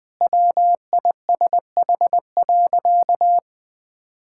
Morse code, heard as a keyed tone, about 20 wpm, 700 Hz. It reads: WISH.
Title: Sentences from Top 1600 Words 20wpm